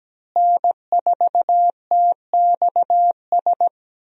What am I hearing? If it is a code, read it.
Morse code, heard as a keyed tone, about 17 wpm, 700 Hz. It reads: N4TXS
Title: USA Call Signs - 5 Characters - Repeated 3x - 17wpm